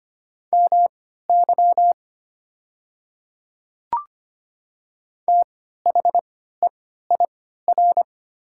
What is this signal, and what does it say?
Morse code, heard as a keyed tone, about 25 wpm, spaced as for 14 wpm, 700 Hz. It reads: MY THEIR
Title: 3x Character Spacing - Top 100 Words - 25wpm